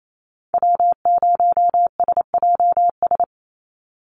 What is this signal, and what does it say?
Morse code, heard as a keyed tone, about 28 wpm, 700 Hz. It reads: W0SJS
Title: Call Signs Repeated 2x - 28wpm